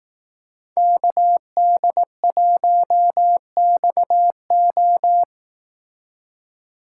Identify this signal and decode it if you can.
Morse code, heard as a keyed tone, about 18 wpm, 700 Hz. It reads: KD1XO
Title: Speed Racing - USA 2x2 Call Signs 15wpm